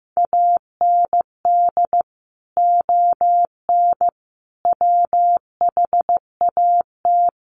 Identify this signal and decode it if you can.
Morse code, heard as a keyed tone, about 15 wpm, 700 Hz. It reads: AND ON WHAT
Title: Sentences from Top 1100 Words 15wpm